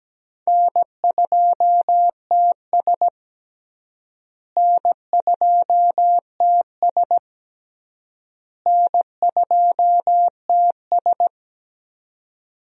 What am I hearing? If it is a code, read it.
Morse code, heard as a keyed tone, about 17 wpm, 700 Hz. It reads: N2TS N2TS N2TS
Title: USA Call Signs - 4 Characters - Repeated 3x - 17wpm